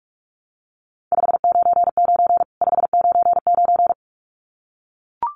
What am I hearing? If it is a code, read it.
Morse code, heard as a keyed tone, about 44 wpm, 700 Hz. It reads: 599 599
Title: Common QSO Phrases 45wpm